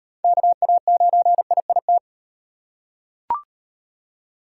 Morse code, heard as a keyed tone, about 38 wpm, 700 Hz. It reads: KA9IIT